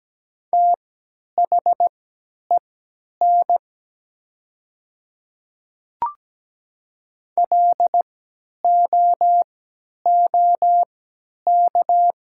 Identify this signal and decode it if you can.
Morse code, heard as a keyed tone, about 17 wpm, spaced as for 10 wpm, 700 Hz. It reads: THEN LOOK
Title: 3x Character Spacing - Top 100 Words - 17wpm